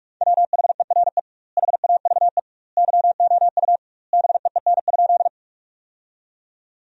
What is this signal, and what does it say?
Morse code, heard as a keyed tone, about 46 wpm, 700 Hz. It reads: WHERE HAVE YOU BEEN?
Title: 2x Word-Spacing - Sentences from Top 200 Words - 45wpm